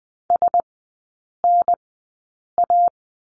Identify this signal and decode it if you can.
Morse code, heard as a keyed tone, about 20 wpm, spaced as for 8 wpm, 700 Hz. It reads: SNA
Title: 035 Lesson 7e - Letter s - Three Characters 20wpm